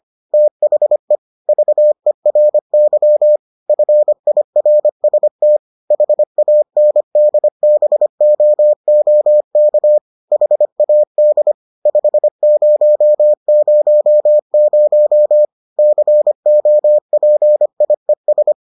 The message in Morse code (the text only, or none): THE VERY FIRST HANDBOOK HAD 5000 COPIES